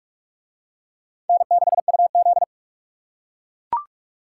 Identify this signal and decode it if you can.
Morse code, heard as a keyed tone, about 44 wpm, 700 Hz. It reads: N6UZ